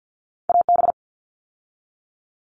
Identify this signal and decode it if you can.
Morse code, heard as a keyed tone, about 50 wpm, 700 Hz. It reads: AB
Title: ARRL Sections 50wpm